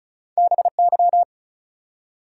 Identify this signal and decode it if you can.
Morse code, heard as a keyed tone, about 33 wpm, 700 Hz. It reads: BY
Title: Two Letter Words - 35wpm